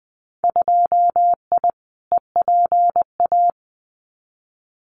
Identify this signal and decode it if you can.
Morse code, heard as a keyed tone, about 20 wpm, 700 Hz. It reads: 2I EPA
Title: Winter Field Day Exchanges 20wpm